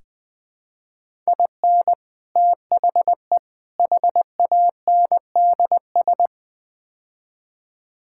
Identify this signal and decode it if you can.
Morse code, heard as a keyed tone, about 20 wpm, 700 Hz. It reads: IN THE HANDS
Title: Build A Phrase - 5-Word Phrases - 20wpm